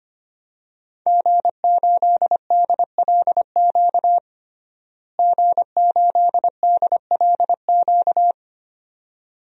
Morse code, heard as a keyed tone, about 25 wpm, 700 Hz. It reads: G8DLQ G8DLQ
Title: DXCC-100 Call Signs - 5 Characters - Repeated 2x - 25wpm